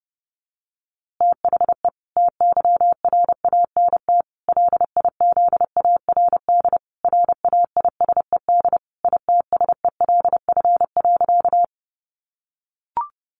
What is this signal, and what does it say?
Morse code, heard as a keyed tone, about 30 wpm, 700 Hz. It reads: THE TYRANT LIZARD RAISED ITSELF.
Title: A Sound Of Thunder 30wpm